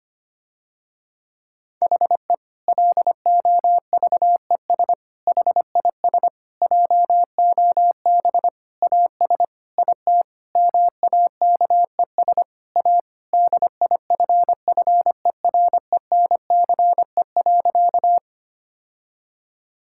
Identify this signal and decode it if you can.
Morse code, heard as a keyed tone, about 25 wpm, 700 Hz. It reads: HE LOVES HIS JOB AS IT MAKES A DIFFERENCE.
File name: practice / 25wpm / Difficult Sentences with Hints - 25wpm